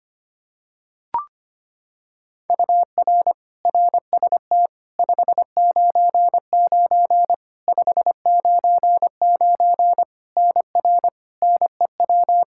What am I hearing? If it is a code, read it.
Morse code, heard as a keyed tone, about 25 wpm, 700 Hz. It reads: UR RST 599 599 NR NEW
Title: Practice QSO - Exchange 1 - Beginning - 25wpm